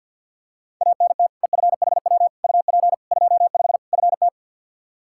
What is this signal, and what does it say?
Morse code, heard as a keyed tone, about 52 wpm, 700 Hz. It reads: ANT EFHW UP 25 FT
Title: Practice QSO - Exchange 2 - Info - 50wpm